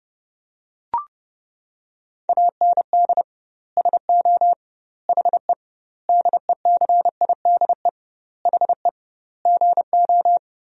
Morse code, heard as a keyed tone, about 30 wpm, 700 Hz. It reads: AND SO HE DECIDE HE GO
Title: 2x Word-Spacing - Sentences from Top 500 Words - 30wpm